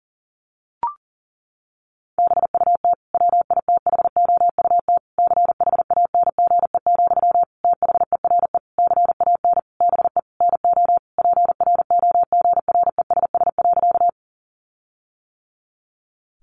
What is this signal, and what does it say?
Morse code, heard as a keyed tone, about 40 wpm, 700 Hz. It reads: BUT WITHOUT CHANGE, THERE CAN BE NO PROGRESS.
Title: Sentences from Top 1400 Words 40wpm